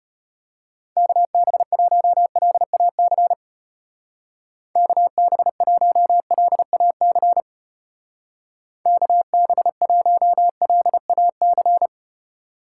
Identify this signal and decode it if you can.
Morse code, heard as a keyed tone, about 31 wpm, 700 Hz. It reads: KB1LAC KB1LAC KB1LAC
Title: Speed Racing - USA 2x3 Call Signs 25wpm